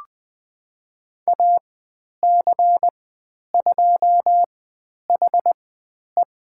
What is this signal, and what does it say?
Morse code, heard as a keyed tone, about 20 wpm, spaced as for 10 wpm, 700 Hz. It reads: AC2HE
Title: 095 Lesson 16g - Letter c - Call Signs 20wpm